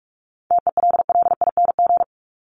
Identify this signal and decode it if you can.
Morse code, heard as a keyed tone, about 44 wpm, 700 Hz. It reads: TELLING
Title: ING Suffix - Top 100 Words with Base Words 45wpm